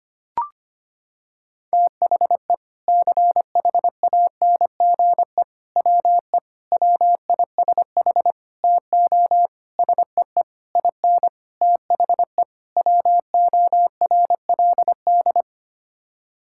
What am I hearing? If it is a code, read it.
Morse code, heard as a keyed tone, about 25 wpm, 700 Hz. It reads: THE CHANGE WE WISH TO SEE IN THE WORLD
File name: Sets of 9 Words - 25wpm